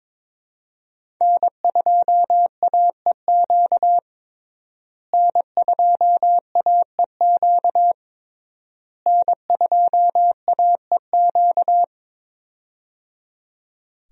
Morse code, heard as a keyed tone, about 22 wpm, 700 Hz. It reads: N2AEQ N2AEQ N2AEQ
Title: USA Call Signs - 5 Characters - Repeated 3x - 22wpm